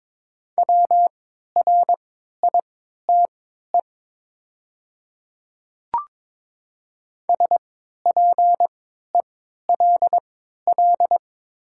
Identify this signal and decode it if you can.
Morse code, heard as a keyed tone, about 23 wpm, spaced as for 13 wpm, 690 Hz. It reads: WRITE SPELL
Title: 3x Character Spacing - Top 200 Words - 22wpm